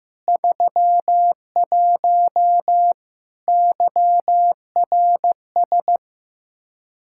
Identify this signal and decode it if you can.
Morse code, heard as a keyed tone, about 15 wpm, 700 Hz. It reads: 31 YRS
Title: Practice QSO - Exchange 3 - More Info - 15wpm